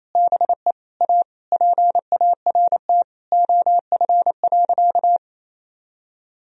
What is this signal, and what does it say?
Morse code, heard as a keyed tone, about 28 wpm, 700 Hz. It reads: BE A PART OF.
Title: Sentences from Top 1000 Words - Review Part 1 - 28wpm